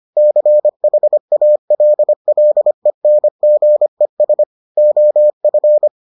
Morse code, heard as a keyed tone, about 25 wpm, 600 Hz. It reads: CHALLENGES OF